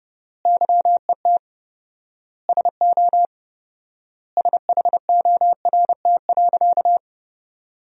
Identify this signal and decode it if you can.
Morse code, heard as a keyed tone, about 31 wpm, 700 Hz. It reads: YET SO SHORT.
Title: 4x Word-Spacing - Sentences from Top 500 Words - 30wpm